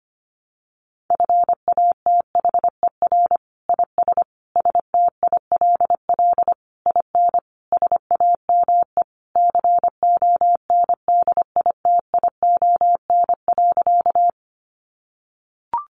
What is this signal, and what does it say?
Morse code, heard as a keyed tone, about 25 wpm, 700 Hz. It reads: FATHER IS STILL IN SAME CONDITION.
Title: Sentences from Top 1000 Words 25wpm